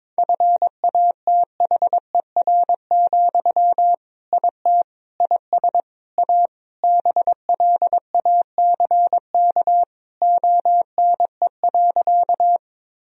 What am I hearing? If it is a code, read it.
Morse code, heard as a keyed tone, about 21 wpm, 700 Hz. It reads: FATHER, IT IS A BLACK ONE.
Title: Sentences from Top 1000 Words - Review Part 2 - 22wpm